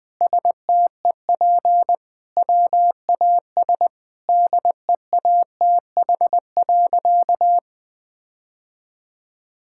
Morse code, heard as a keyed tone, about 20 wpm, 700 Hz. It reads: STEP WAS DEATH.
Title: Sentences from Top 1600 Words 20wpm